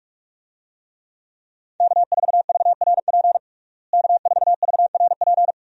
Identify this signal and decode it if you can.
Morse code, heard as a keyed tone, about 46 wpm, 700 Hz. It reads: K4VRP K4VRP
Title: USA Call Signs - 5 Characters - Repeated 2x - 45wpm